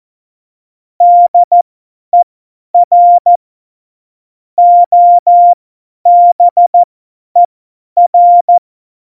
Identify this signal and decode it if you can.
Morse code, heard as a keyed tone, about 14 wpm, spaced as for 10 wpm, 700 Hz. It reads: DER OBER
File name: Morsekurs / Witze / Witz 17.Dezember 2019, 50 F70